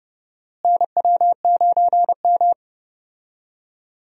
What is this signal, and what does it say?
Morse code, heard as a keyed tone, about 30 wpm, 700 Hz. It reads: NW9M
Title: USA Call Signs - 4 Characters - Repeated 2x - 30wpm